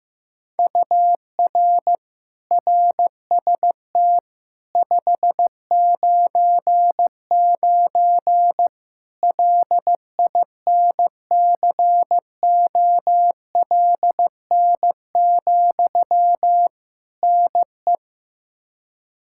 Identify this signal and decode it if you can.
Morse code, heard as a keyed tone, about 15 wpm, 700 Hz. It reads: UR RST 599 LINCOLN, NE
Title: Practice QSO - Exchange 1 - Beginning - 15wpm